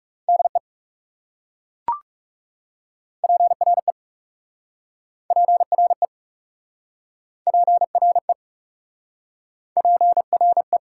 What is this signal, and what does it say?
Morse code, heard as a keyed tone, about 40 wpm, 700 Hz. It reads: DE PRE PRE PRE PRE